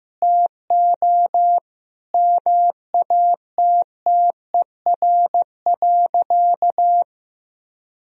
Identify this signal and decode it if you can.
Morse code, heard as a keyed tone, about 15 wpm, 700 Hz. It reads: TO MATTER.
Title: Sentences from Top 1000 Words - Review Part 1 - 15wpm